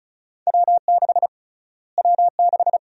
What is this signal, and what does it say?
Morse code, heard as a keyed tone, about 36 wpm, 700 Hz. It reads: W6 W6
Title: USA Call Sign Prefixes - Repeated 2x - 35wpm